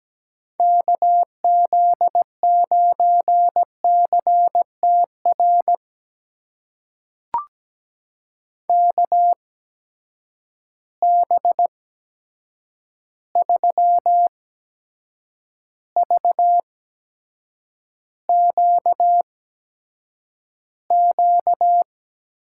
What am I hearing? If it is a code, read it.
Morse code, heard as a keyed tone, about 17 wpm, spaced as for 7 wpm, 700 Hz. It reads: KZ9CTR KB3VQQ
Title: USA Call Signs - 6 Characters - 8x Character Spacing - 17wpm